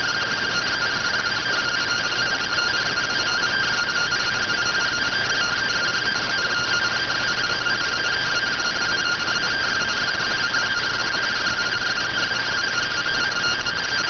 Two-tone teleprinter signal, about 170 Hz shift.